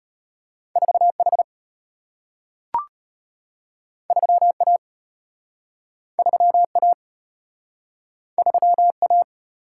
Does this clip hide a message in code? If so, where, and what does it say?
Yes: Morse code from 0.8 to 9.2 s, heard as a keyed tone, about 34 wpm, 700 Hz. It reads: 4 H 3 A 3 A 3 A